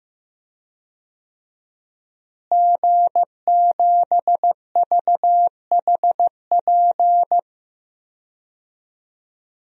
Morse code, heard as a keyed tone, about 15 wpm, 700 Hz. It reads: G7VHP
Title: DXCC-100 Call Signs - 5 Characters - Repeated 2x - 15wpm